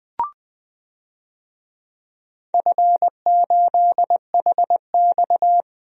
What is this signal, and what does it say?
Morse code, heard as a keyed tone, about 20 wpm, 700 Hz. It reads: F8HX
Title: DXCC-100 Call Signs - 4 Characters - Repeated 2x - 20wpm